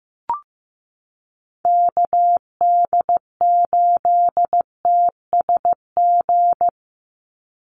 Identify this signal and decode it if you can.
Morse code, heard as a keyed tone, about 15 wpm, 700 Hz. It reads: KD8TSG
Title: USA Call Signs 15wpm